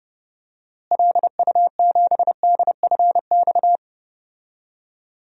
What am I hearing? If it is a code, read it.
Morse code, heard as a keyed tone, about 30 wpm, 700 Hz. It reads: LU7DFX
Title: DXCC-100 Call Signs - 6 Characters - Repeated 2x - 30wpm